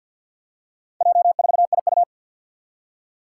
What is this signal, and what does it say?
Morse code, heard as a keyed tone, about 50 wpm, 700 Hz. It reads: J4IU
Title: Four Challenging Letter-Number Combinations 50wpm